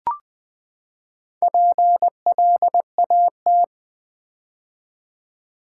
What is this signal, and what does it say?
Morse code, heard as a keyed tone, about 20 wpm, 700 Hz. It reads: PLAT
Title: Build A Word - 4 Letters At A Time - 8-Letter Words - 20wpm